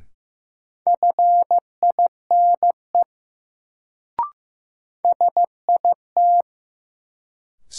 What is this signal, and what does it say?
Morse code, heard as a keyed tone, about 15 wpm, 700 Hz. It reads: FINE SIT
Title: Sentences from Top 500 Words - New Word Drill - 15wpm